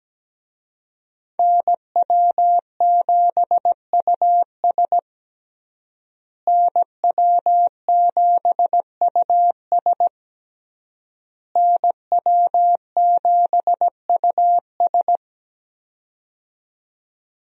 Morse code, heard as a keyed tone, about 17 wpm, 700 Hz. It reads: NW7US NW7US NW7US
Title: USA Call Signs - 5 Characters - Repeated 3x - 17wpm